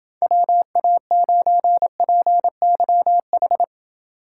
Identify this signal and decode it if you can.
Morse code, heard as a keyed tone, about 29 wpm, 700 Hz. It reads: WA9PYH